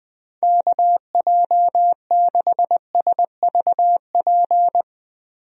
Morse code, heard as a keyed tone, about 20 wpm, 700 Hz. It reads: KJ6SVP
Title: USA Call Signs - 6 Characters - Repeated 2x - 20wpm